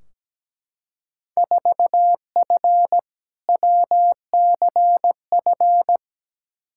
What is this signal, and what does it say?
Morse code, heard as a keyed tone, about 17 wpm, 700 Hz. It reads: 4F WCF
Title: Field Day Exchanges - 17wpm